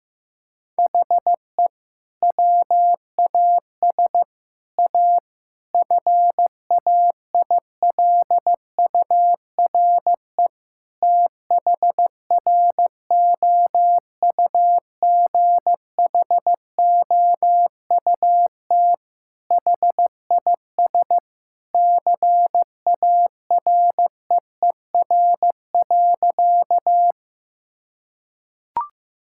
Morse code, heard as a keyed tone, about 15 wpm, 700 Hz. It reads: HE WAS A FAILURE THROUGHOUT HIS CAREER.
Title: Sentences from Top 1000 Words - Review Part 6 - 15wpm